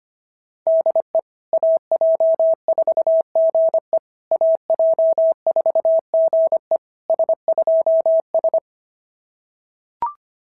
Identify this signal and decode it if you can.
Morse code, heard as a keyed tone, about 24 wpm, 650 Hz. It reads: DE AJ4GE AJ4GE S2S
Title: SOTA QSO Practice 25wpm